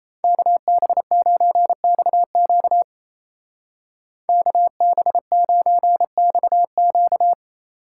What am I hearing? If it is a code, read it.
Morse code, heard as a keyed tone, about 36 wpm, 700 Hz. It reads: KB9XQ KB9XQ